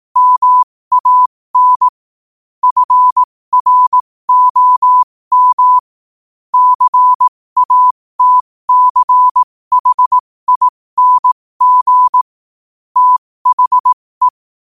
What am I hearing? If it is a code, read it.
Morse code, heard as a keyed tone, about 18 wpm, spaced as for 15 wpm, 1000 Hz. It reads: MAN FROM CATCHING THE
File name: qotdfiles / 15wpm